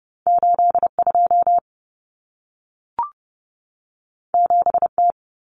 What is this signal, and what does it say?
Morse code, heard as a keyed tone, about 30 wpm, 700 Hz. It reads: 82 7T
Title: Common Cut Numbers - 2-Digit 30wpm